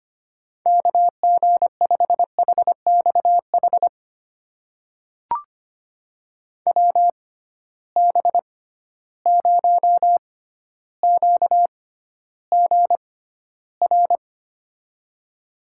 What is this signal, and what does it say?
Morse code, heard as a keyed tone, about 25 wpm, spaced as for 13 wpm, 700 Hz. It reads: KG5HXH WB0QGR
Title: USA Call Signs - 6 Characters - 6x Character Spacing - 25wpm